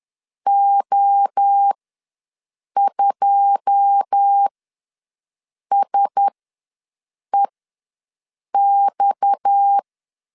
Morse code, heard as a keyed tone, about 10 wpm, spaced as for 6 wpm, 800 Hz. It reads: O2SEX
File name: LEZIONI / 9 2 ws 10wpm 4130